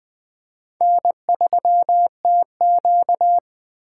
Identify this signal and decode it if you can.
Morse code, heard as a keyed tone, about 20 wpm, 700 Hz. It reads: N3TQ
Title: Speed Racing - USA 1x2 Call Signs 17wpm